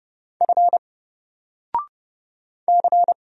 Morse code, heard as a keyed tone, about 29 wpm, 700 Hz. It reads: F C
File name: Single-Letter-Number-30wpm